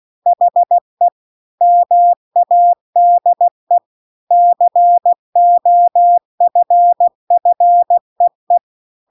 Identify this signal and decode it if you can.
Morse code, heard as a keyed tone, about 16 wpm, 700 Hz. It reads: HE MADE COFFEE